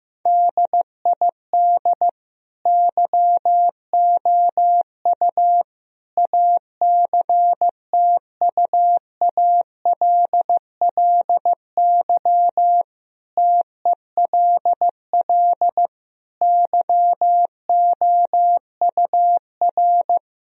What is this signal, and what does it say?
Morse code, heard as a keyed tone, about 15 wpm, 700 Hz. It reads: DID YOU ACTUALLY TELL YOUR
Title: Sentences from Top 1000 Words - Review Part 4 - 15wpm